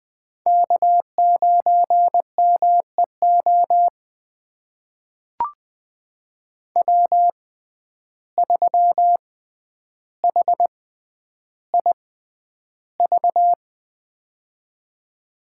Morse code, heard as a keyed tone, about 20 wpm, spaced as for 10 wpm, 700 Hz. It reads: K9MEO W3HIV
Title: USA Call Signs - 5 Characters - 6x Character Spacing - 20wpm